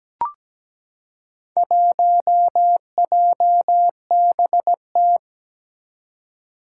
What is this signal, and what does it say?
Morse code, heard as a keyed tone, about 17 wpm, 700 Hz. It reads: 1JBT
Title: Four Challenging Letter-Number Combinations 17wpm